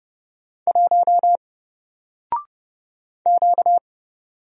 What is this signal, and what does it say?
Morse code, heard as a keyed tone, about 30 wpm, 700 Hz. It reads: 1 Q
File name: Single-Letter-Number-30wpm